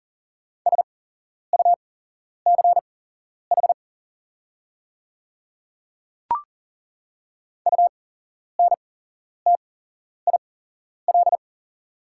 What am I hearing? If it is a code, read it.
Morse code, heard as a keyed tone, about 39 wpm, spaced as for 11 wpm, 690 Hz. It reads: SUCH UNTIL
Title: 8x Character Spacing - Top 500 Words - 40wpm